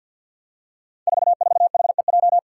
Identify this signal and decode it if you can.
Morse code, heard as a keyed tone, about 46 wpm, 700 Hz. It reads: 44HEJ
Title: Five Challenging Letter-Number Combinations 50wpm